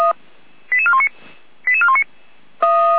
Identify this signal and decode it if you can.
Five-tone selective call (ZVEI), heard as quick runs of five tones, tones of about 70 ms. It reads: 88419 88419